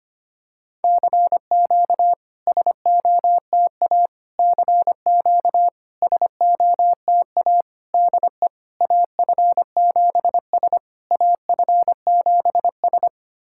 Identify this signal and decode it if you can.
Morse code, heard as a keyed tone, about 25 wpm, 700 Hz. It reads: CQ SOTA CQ SOTA DE AF7S AF7S
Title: SOTA QSO Practice 25wpm